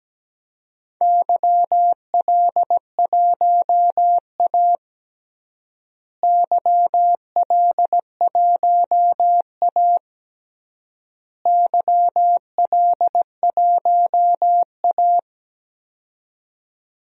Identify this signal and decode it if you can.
Morse code, heard as a keyed tone, about 17 wpm, 700 Hz. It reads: YL1A YL1A YL1A
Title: DXCC-100 Call Signs - 4 Characters - Repeated 3x - 17wpm